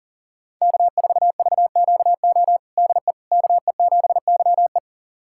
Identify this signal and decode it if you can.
Morse code, heard as a keyed tone, about 37 wpm, 700 Hz. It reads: K4VQO DE KE7YE
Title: USA Callsign DE Callsign - 5 Characters - 40wpm